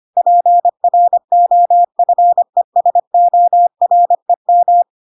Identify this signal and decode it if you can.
Morse code, heard as a keyed tone, about 25 wpm, 700 Hz. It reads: PROFESOREM